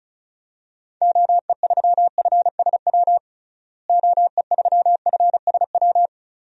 Morse code, heard as a keyed tone, about 34 wpm, 700 Hz. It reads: OE3FSW OE3FSW
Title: DXCC-100 Call Signs - 6 Characters - Repeated 3x - 35wpm